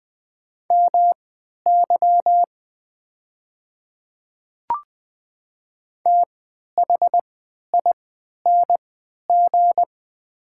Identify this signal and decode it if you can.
Morse code, heard as a keyed tone, about 20 wpm, spaced as for 11 wpm, 700 Hz. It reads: MY THING
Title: 3x Character Spacing - Top 100 Words - 20wpm